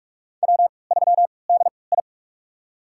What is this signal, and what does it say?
Morse code, heard as a keyed tone, about 44 wpm, spaced as for 25 wpm, 700 Hz. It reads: W3DI
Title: USA Call Signs - 4 Characters - 3x Character Spacing - 45wpm